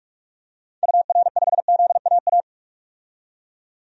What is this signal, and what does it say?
Morse code, heard as a keyed tone, about 41 wpm, 700 Hz. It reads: UR5ZAA
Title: DXCC-100 Call Signs - 6 Characters - 45wpm